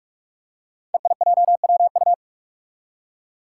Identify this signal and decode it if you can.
Morse code, heard as a keyed tone, about 42 wpm, 700 Hz. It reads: EIJWU
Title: Five Challenging Letter-Number Combinations 45wpm